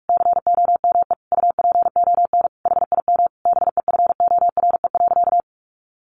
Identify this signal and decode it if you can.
Morse code, heard as a keyed tone, about 44 wpm, 700 Hz. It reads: COME UPON HIM BEFORE.
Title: Sentences from Top 1100 Words 45wpm